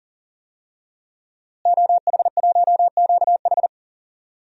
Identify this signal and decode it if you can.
Morse code, heard as a keyed tone, about 40 wpm, 700 Hz. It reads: OH1QH
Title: DXCC-100 Call Signs - 5 Characters - Repeated 3x - 40wpm